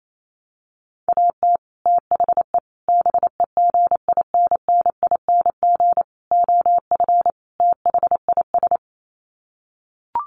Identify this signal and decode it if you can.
Morse code, heard as a keyed tone, about 28 wpm, 700 Hz. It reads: AT THE BEGINNING OF THIS
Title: Sets of 5 Words Encore 28wpm